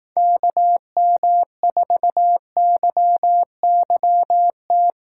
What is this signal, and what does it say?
Morse code, heard as a keyed tone, about 19 wpm, 700 Hz. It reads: KM4YYT
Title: Speed Racing - USA 2x3 Call Signs 15wpm